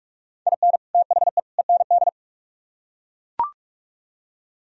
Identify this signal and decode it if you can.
Morse code, heard as a keyed tone, about 42 wpm, 700 Hz. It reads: IN THE END